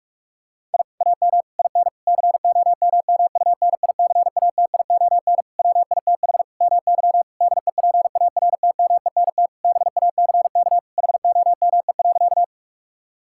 Morse code, heard as a keyed tone, about 46 wpm, 700 Hz. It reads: I AM IN COMMUNICATION WITH MY DEPARTMENT BACK HOME.
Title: Sentences from Top 1000 Words - Review Part 6 - 45wpm